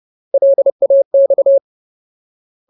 Morse code, heard as a keyed tone, about 30 wpm, 550 Hz. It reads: LAX
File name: Ultimate Word List / 30WPM Ultimate